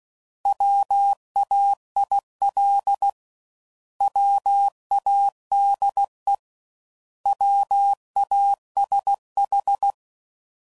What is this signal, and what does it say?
Morse code, heard as a keyed tone, about 16 wpm, 800 Hz. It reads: WAIL WADE WASH
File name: Prefixes / By First Letter / Ws